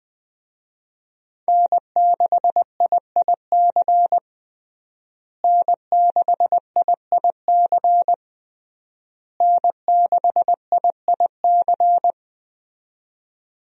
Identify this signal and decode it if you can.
Morse code, heard as a keyed tone, about 21 wpm, 700 Hz. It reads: N6IIC N6IIC N6IIC
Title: USA Call Signs - 5 Characters - Repeated 3x - 20wpm